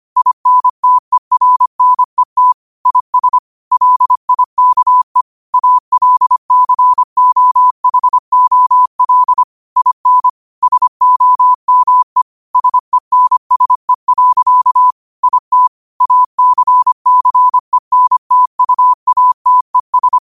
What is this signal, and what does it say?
Morse code, heard as a keyed tone, about 25 wpm, 1000 Hz. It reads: INTERNET IS LIKE ALCOHOL IN SOME SENSE. IT ACCENTUATES
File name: qotdfiles / 25wpm